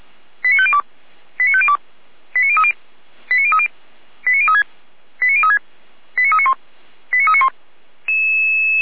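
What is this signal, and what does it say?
Five-tone selective call (ZVEI), heard as quick runs of five tones, tones of about 70 ms. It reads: 89502 89502 89030 89030 89036 89036 89391 89391
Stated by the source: ZVEI five-tone call 89042 received on an SDR.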